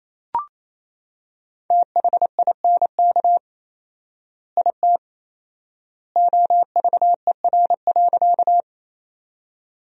Morse code, heard as a keyed tone, about 28 wpm, 700 Hz. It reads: THINK IT OVER.
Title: 4x Word-Spacing - Sentences from Top 200 Words - 28wpm